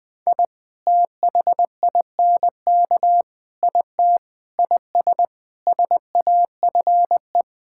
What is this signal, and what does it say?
Morse code, heard as a keyed tone, about 20 wpm, 700 Hz. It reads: I THINK IT IS SAFE
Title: Sentences with 7-Word Phrases - 20wpm